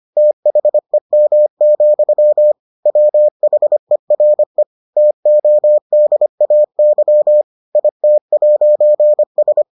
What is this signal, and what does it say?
Morse code, heard as a keyed tone, about 25 wpm, 600 Hz. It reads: THEM, WHERE TODAY IT'S